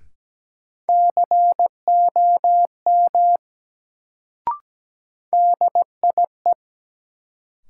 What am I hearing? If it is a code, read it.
Morse code, heard as a keyed tone, about 17 wpm, 700 Hz. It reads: COM DIE
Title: Sentences from Top 1300 Words - New Word Drill - 17wpm